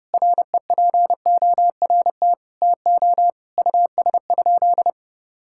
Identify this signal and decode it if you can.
Morse code, heard as a keyed tone, about 30 wpm, 700 Hz. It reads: REPORT TO US?
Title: Sentences from Top 1000 Words - Review Part 6 - 30wpm